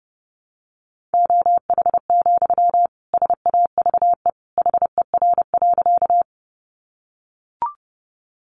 Morse code, heard as a keyed tone, about 28 wpm, 700 Hz. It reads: OH, SAVE HER.
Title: Sentences from Top 1100 Words 30wpm